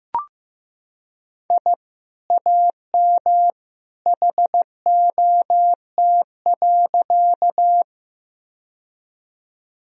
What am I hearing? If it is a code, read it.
Morse code, heard as a keyed tone, about 15 wpm, 700 Hz. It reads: I AM HOT.
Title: Sentences from Top 1000 Words - Review Part 3 - 15wpm